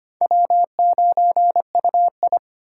Morse code, heard as a keyed tone, about 25 wpm, 700 Hz. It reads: W9UI
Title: USA Call Signs - 4 Characters - 6x Character Spacing - 25wpm